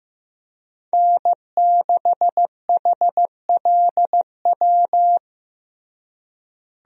Morse code, heard as a keyed tone, about 15 wpm, 700 Hz. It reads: N6HLW
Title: USA Call Signs - 5 Characters - Repeated 3x - 15wpm